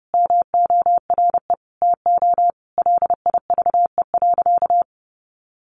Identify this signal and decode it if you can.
Morse code, heard as a keyed tone, about 30 wpm, 700 Hz. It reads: MORE TO LIVE.
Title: Sentences from Top 300 Words 30wpm